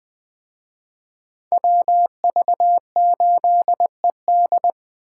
Morse code, heard as a keyed tone, about 20 wpm, 700 Hz. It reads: WV8ED